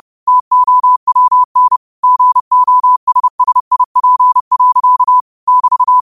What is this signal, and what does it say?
Morse code, heard as a keyed tone, about 30 wpm, 1000 Hz. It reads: TOWN GOSSIP. <BT>
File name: qotdfiles / 30wpm